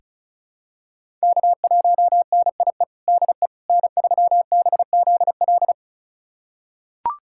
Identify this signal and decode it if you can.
Morse code, heard as a keyed tone, about 34 wpm, 700 Hz. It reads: K1NIE DE N3BZL
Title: USA Callsign DE Callsign - 5 Characters - 35wpm